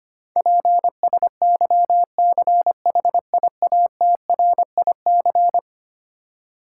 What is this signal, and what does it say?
Morse code, heard as a keyed tone, about 25 wpm, 700 Hz. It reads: PSYCHIATRIC
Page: morse-code-ninja-practice-25wpm